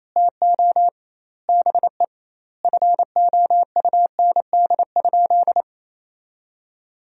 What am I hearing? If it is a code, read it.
Morse code, heard as a keyed tone, about 28 wpm, 700 Hz. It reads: TO BE FOUND?
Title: 2x Word-Spacing - Sentences from Top 300 Words - 28wpm